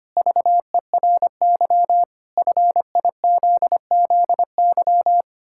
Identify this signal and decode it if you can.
Morse code, heard as a keyed tone, about 25 wpm, 700 Hz. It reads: VERY FIZZY
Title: Pangram Challenge v2 - 25wpm